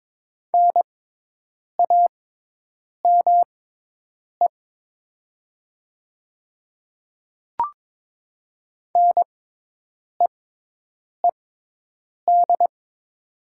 Morse code, heard as a keyed tone, about 22 wpm, spaced as for 8 wpm, 700 Hz. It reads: NAME NEED
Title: 6x Character Spacing - Top 200 Words - 22wpm